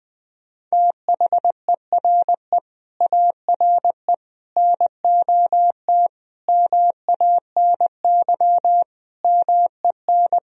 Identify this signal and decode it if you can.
Morse code, heard as a keyed tone, about 20 wpm, 700 Hz. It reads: THERE ARE NOT MANY MEN
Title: Sentences from Top 1000 Words - Review Part 1 - 20wpm